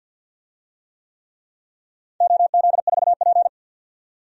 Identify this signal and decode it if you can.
Morse code, heard as a keyed tone, about 52 wpm, 700 Hz. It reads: OZ4P
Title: DXCC-100 Call Signs - 4 Characters - 50wpm